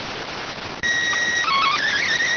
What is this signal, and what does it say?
SSTV (slow-scan television), PD 290.